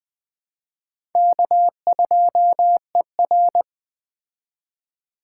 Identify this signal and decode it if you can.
Morse code, heard as a keyed tone, about 20 wpm, 700 Hz. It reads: K2ER